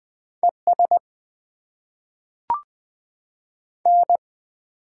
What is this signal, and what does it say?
Morse code, heard as a keyed tone, about 20 wpm, 700 Hz. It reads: E S N
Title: Top 20 QSO Elements - 8x Character Spacing - 20wpm